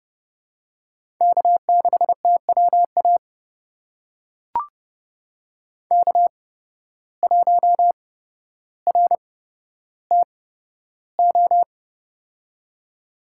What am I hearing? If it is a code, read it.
Morse code, heard as a keyed tone, about 30 wpm, spaced as for 13 wpm, 700 Hz. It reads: K6TWA K1RTO
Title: USA Call Signs - 5 Characters - 8x Character Spacing - 30wpm